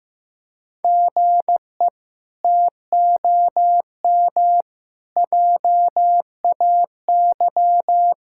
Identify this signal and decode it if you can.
Morse code, heard as a keyed tone, about 15 wpm, 700 Hz. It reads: GE TOM JAY
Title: K1USN SST QSO Practice - 15wpm